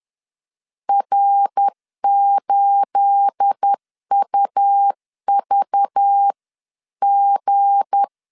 Morse code, heard as a keyed tone, about 10 wpm, 800 Hz. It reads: R8UVG
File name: LEZIONI / 9 28 10wpm 2842